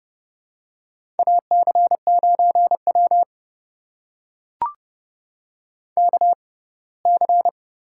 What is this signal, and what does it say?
Morse code, heard as a keyed tone, about 30 wpm, spaced as for 20 wpm, 700 Hz. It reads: AC9W KC
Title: USA Call Signs - 6x Character Spacing - 30wpm